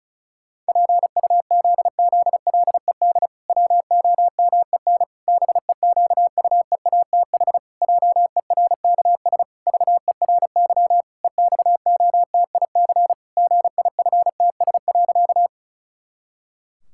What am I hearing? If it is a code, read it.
Morse code, heard as a keyed tone, about 36 wpm, 700 Hz. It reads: PUZZLED WOMEN BEQUEATH JERKS VERY EXOTIC GIFTS.
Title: Pangram Challenge v2 - 35wpm